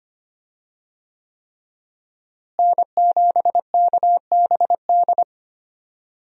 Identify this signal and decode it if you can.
Morse code, heard as a keyed tone, about 25 wpm, 700 Hz. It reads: N7KBD
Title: USA Call Signs - 5 Characters - Repeated 3x - 25wpm